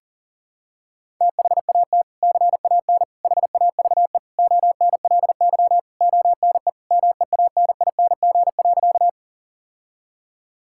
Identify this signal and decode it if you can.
Morse code, heard as a keyed tone, about 40 wpm, 700 Hz. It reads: THAT CAN HAVE ONLY ONE MEANING.
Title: Sentences from Top 1000 Words - Review Part 6 - 40wpm